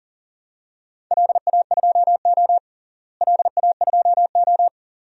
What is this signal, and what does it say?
Morse code, heard as a keyed tone, about 40 wpm, 700 Hz. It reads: LA2O LA2O
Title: DXCC-100 Call Signs - 4 Characters - Repeated 2x - 40wpm